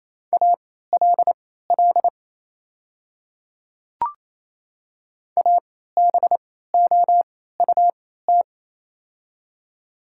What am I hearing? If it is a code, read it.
Morse code, heard as a keyed tone, about 28 wpm, spaced as for 16 wpm, 700 Hz. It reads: ALL ABOUT
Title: 3x Character Spacing - Top 100 Words - 28wpm